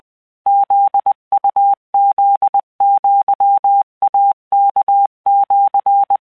Morse code (Japanese Wabun code), heard as a keyed tone, about 20 wpm, 800 Hz. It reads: フウフアイワシ
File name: keuikuchiyokugo20wptan